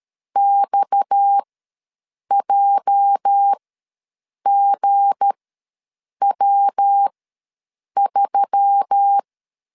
Morse code, heard as a keyed tone, about 13 wpm, spaced as for 7 wpm, 800 Hz. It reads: XJGW3
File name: LEZIONI / 10 SIGNS / 10A SLASH ws 12wpm 3443